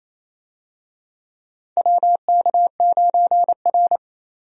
Morse code, heard as a keyed tone, about 28 wpm, 700 Hz. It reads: WK9R
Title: USA Call Signs - 4 Characters - 28wpm